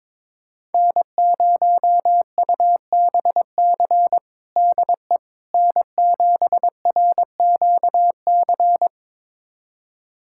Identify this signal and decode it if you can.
Morse code, heard as a keyed tone, about 22 wpm, 700 Hz. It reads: N0UBC DE N7RQC
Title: USA Callsign DE Callsign - 5 Characters - 22wpm